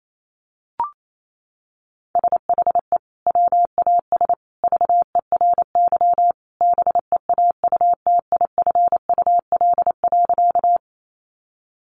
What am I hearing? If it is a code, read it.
Morse code, heard as a keyed tone, about 28 wpm, 700 Hz. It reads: SHE WAS VERY BEAUTIFUL.
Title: Sentences from Top 900 Words 28wpm